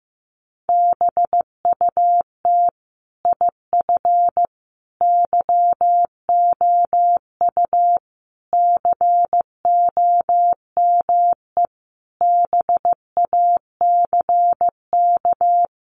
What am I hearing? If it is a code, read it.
Morse code, heard as a keyed tone, about 15 wpm, 700 Hz. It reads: BUT IF YOU COME BACK
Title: A Sound Of Thunder 15wpm